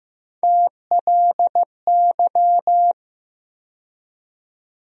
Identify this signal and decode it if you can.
Morse code, heard as a keyed tone, about 15 wpm, 700 Hz. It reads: TLY
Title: USA Call Sign Postfixes - 3 Letters 15wpm